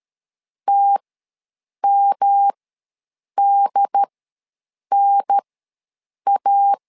Morse code, heard as a keyed tone, about 13 wpm, spaced as for 7 wpm, 800 Hz. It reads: TMDNA
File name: LEZIONI / 3 MWRO / 3 M ws 12wpm 1745